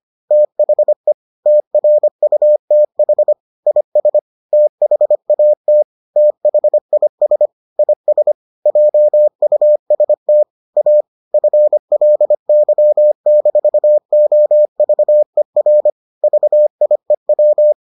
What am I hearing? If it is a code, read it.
Morse code, heard as a keyed tone, about 25 wpm, 600 Hz. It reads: THE TRUTH IS THAT THIS IS JUST A FLY-OVER VIEW